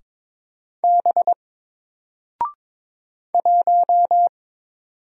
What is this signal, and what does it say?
Morse code, heard as a keyed tone, about 22 wpm, 700 Hz. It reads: B 1
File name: Single-Letter-Number-22wpm